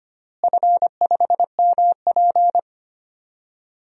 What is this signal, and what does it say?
Morse code, heard as a keyed tone, about 25 wpm, 700 Hz. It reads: F5MP
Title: DXCC-100 Call Signs - 4 Characters - Repeated 2x - 25wpm